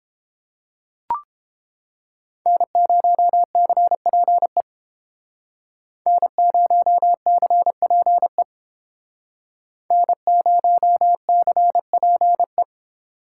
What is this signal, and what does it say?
Morse code, heard as a keyed tone, about 30 wpm, 700 Hz. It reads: N0CPE N0CPE N0CPE